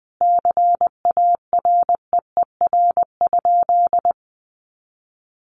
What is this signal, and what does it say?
Morse code, heard as a keyed tone, about 20 wpm, 700 Hz. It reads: CAREER?
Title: Sentences from Top 1000 Words 20wpm